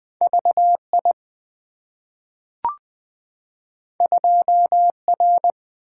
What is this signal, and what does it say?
Morse code, heard as a keyed tone, about 20 wpm, 700 Hz. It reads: VI 2R
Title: Two Letter-Number Combinations - 20wpm